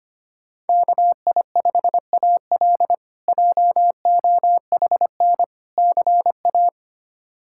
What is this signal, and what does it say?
Morse code, heard as a keyed tone, about 25 wpm, 700 Hz. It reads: KI5AL JOHN CA
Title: K1USN SST QSO Practice - 25wpm